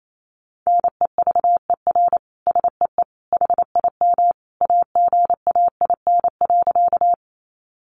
Morse code, heard as a keyed tone, about 28 wpm, 700 Hz. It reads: NEVER SEE HIM AGAIN.
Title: Sentences from Top 300 Words 28wpm